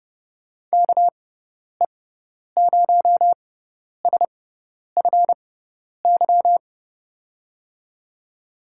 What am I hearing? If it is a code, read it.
Morse code, heard as a keyed tone, about 30 wpm, spaced as for 10 wpm, 700 Hz. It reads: KE0SFY
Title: USA Call Signs - 6 Characters - 6x Character Spacing - 30wpm